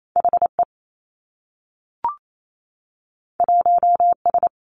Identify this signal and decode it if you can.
Morse code, heard as a keyed tone, about 28 wpm, 700 Hz. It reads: H E 1 S
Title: Two Challenging Letter-Number Combinations 28wpm